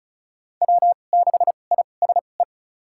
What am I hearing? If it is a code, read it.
Morse code, heard as a keyed tone, about 36 wpm, spaced as for 25 wpm, 700 Hz. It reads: W6ISE
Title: USA Call Signs - 5 Characters - 2x Character Spacing - 35wpm